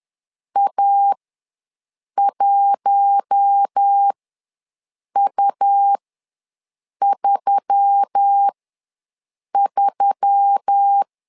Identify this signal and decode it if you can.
Morse code, heard as a keyed tone, about 10 wpm, spaced as for 6 wpm, 800 Hz. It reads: A1U33
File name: LEZIONI / 4 Q1S3 ws 10wpm 2437